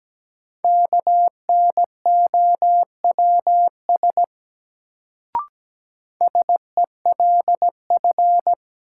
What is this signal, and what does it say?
Morse code, heard as a keyed tone, about 17 wpm, 700 Hz. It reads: KNOWS SELF
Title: Sentences from Top 1000 Words - Review Part 1 - New Word Drill - 17wpm